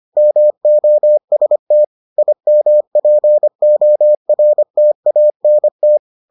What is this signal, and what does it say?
Morse code, heard as a keyed tone, about 25 wpm, 600 Hz. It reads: MOST IMPORTANT